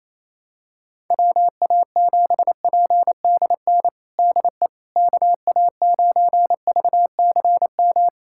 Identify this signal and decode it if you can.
Morse code, heard as a keyed tone, about 28 wpm, 700 Hz. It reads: WA7PDN DE KA9VCM
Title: USA Callsign DE Callsign 28wpm